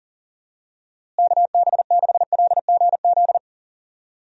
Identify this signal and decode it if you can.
Morse code, heard as a keyed tone, about 40 wpm, 700 Hz. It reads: KB6LGZ
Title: USA Call Signs - 6 Characters - Repeated 2x - 40wpm